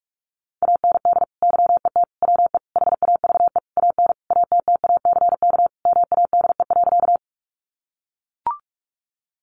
Morse code, heard as a keyed tone, about 44 wpm, 700 Hz. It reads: AND YET WE HAVE AN ATTACK MADE.
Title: Sentences from Top 1500 Words 45wpm